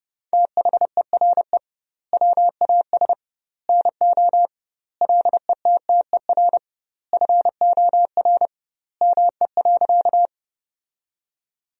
Morse code, heard as a keyed tone, about 30 wpm, 700 Hz. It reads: THERE WAS NO LETTER FOR ME.
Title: 2x Word-Spacing - Sentences from Top 300 Words - 30wpm